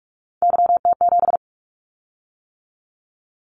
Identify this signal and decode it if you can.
Morse code, heard as a keyed tone, about 46 wpm, 700 Hz. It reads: YT7
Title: Callsign Prefixes for Top 25 DXCC - 45wpm